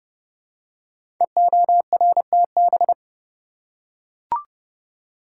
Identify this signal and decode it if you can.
Morse code, heard as a keyed tone, about 30 wpm, 700 Hz. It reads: EORTB
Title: Code Groups - 5 Letter:Numbers - 30wpm